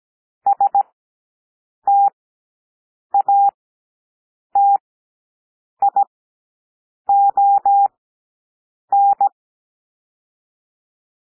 Morse code, heard as a keyed tone, about 17 wpm, spaced as for 7 wpm, 800 Hz. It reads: STATION
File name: SOUND PLAIN 17 wpm